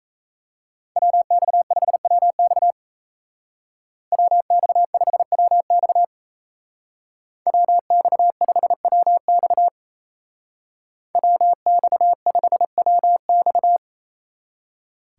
Morse code, heard as a keyed tone, about 37 wpm, 700 Hz. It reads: WX5WX WX5WX WX5WX WX5WX